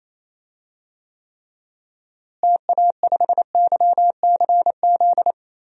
Morse code, heard as a keyed tone, about 28 wpm, 700 Hz. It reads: TA5YCZ
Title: DXCC-100 Call Signs - 6 Characters - Repeated 3x - 28wpm